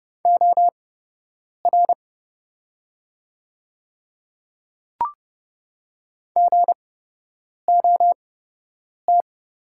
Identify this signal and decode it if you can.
Morse code, heard as a keyed tone, about 30 wpm, spaced as for 8 wpm, 700 Hz. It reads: OR GOT
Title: 8x Character Spacing - Top 300 Words - 30wpm